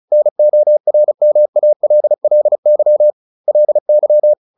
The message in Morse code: NORMALLY LY

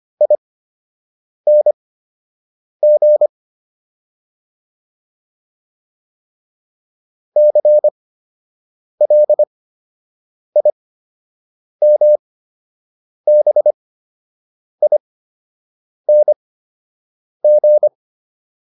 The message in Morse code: ING CLIMBING

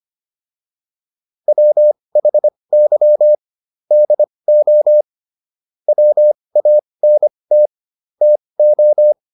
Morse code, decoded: WHY DO WANT TO

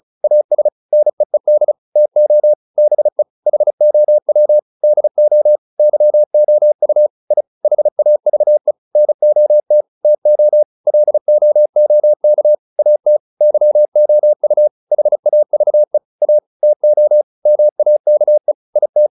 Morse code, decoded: AS NEED TO BE HOW DO YOU I HAVE NOT TO LOOK AT YOU HAVE A TO MAKE IT

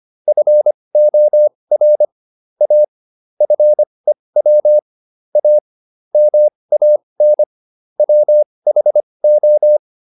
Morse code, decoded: FOR A FEW A MAN WHO

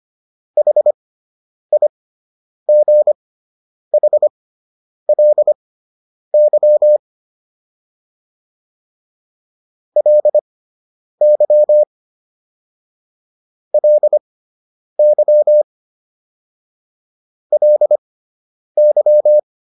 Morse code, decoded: HIGHLY LY LY LY

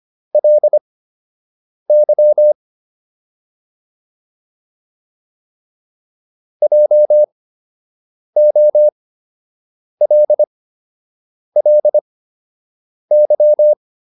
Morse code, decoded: LY JOLLY

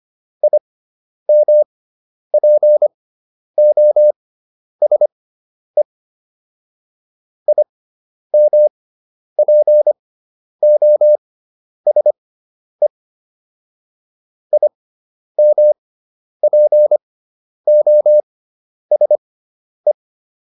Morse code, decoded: IMPOSE IMPOSE IMPOSE

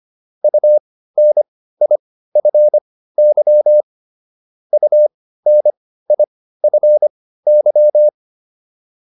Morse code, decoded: UNIFY UNIFY

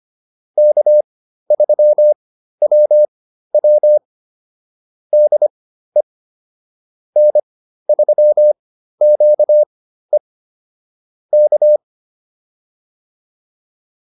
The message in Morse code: K3WW DE N3QE K

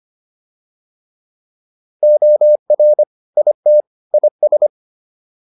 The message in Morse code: OR IT IS

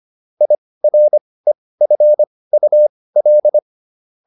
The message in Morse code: IREFUL

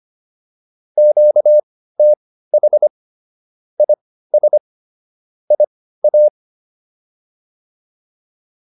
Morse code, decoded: QTH IS IA

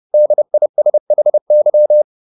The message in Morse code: DISHY

